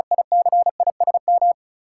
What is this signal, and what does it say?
Morse code, heard as a keyed tone, about 34 wpm, 700 Hz. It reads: ICISM
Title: Sentences from Top 1300 Words - New Word Drill Alt - 35wpm